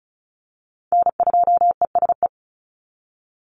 Morse code, heard as a keyed tone, about 34 wpm, 700 Hz. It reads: N2ESE
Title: NA SOTA Chaser Callsigns 35wpm